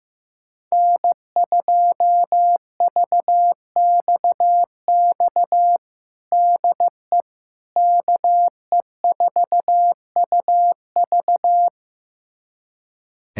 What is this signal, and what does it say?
Morse code, heard as a keyed tone, about 15 wpm, 700 Hz. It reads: N2VXX DE KE4UV
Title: USA Callsign DE Callsign - 5 Characters - 15wpm